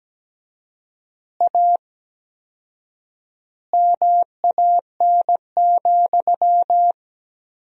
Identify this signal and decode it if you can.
Morse code, heard as a keyed tone, about 17 wpm, 700 Hz. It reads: A MAN,
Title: 4x Word-Spacing - Sentences from Top 200 Words - 17wpm